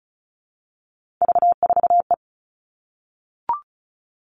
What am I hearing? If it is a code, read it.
Morse code, heard as a keyed tone, about 36 wpm, 700 Hz. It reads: V4E
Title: Three Challenging Letter-Number Combinations 35wpm